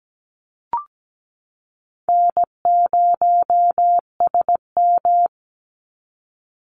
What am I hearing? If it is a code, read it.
Morse code, heard as a keyed tone, about 17 wpm, 700 Hz. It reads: N0SM
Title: Call Signs - 17wpm